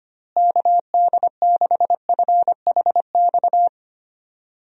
Morse code, heard as a keyed tone, about 25 wpm, 700 Hz. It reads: KD6FHX
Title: USA Call Signs - 6 Characters - Repeated 2x - 25wpm